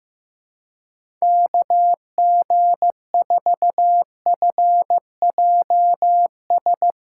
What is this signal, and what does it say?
Morse code, heard as a keyed tone, about 15 wpm, 700 Hz. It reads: KG4FJS
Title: USA Callsign DE Callsign 15wpm